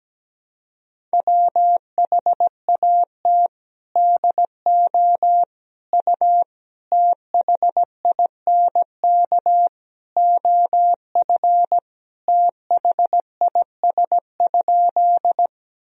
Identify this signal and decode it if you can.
Morse code, heard as a keyed tone, about 17 wpm, 700 Hz. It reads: WHAT DO U THINK OF THIS?
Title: Sentences from Top 200 Words and Abbreviations 17wpm